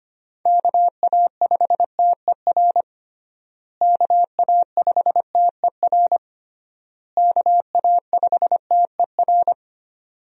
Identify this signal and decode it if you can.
Morse code, heard as a keyed tone, about 25 wpm, 700 Hz. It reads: KA5TER KA5TER KA5TER
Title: USA Call Signs - 6 Characters - Repeated 3x - 25wpm